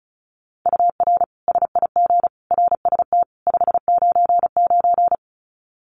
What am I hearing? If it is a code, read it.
Morse code, heard as a keyed tone, about 34 wpm, 700 Hz. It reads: UR SIG RST 599
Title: Common QSO Phrases 35wpm